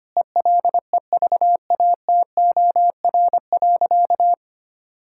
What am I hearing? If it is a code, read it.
Morse code, heard as a keyed tone, about 25 wpm, 700 Hz. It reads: ELEVATOR.
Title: Sentences from Top 1600 Words 25wpm